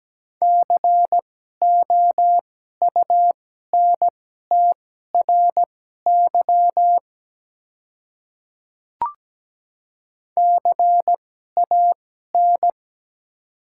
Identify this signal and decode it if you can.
Morse code, heard as a keyed tone, about 17 wpm, spaced as for 12 wpm, 700 Hz. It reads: COUNTRY CAN
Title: 2x Character Spacing - Top 300 Words - 17wpm